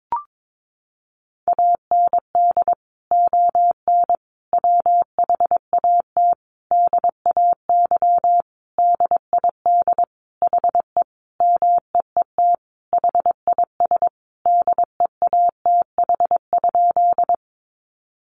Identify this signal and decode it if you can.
Morse code, heard as a keyed tone, about 22 wpm, 700 Hz. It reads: AND ON WHAT DAY DID HE MEET HIS DEATH?
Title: Sentences from Top 1100 Words 22wpm